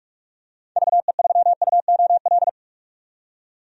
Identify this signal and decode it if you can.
Morse code, heard as a keyed tone, about 47 wpm, 690 Hz. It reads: VE3UOL
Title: DXCC-100 Call Signs - 6 Characters - 45wpm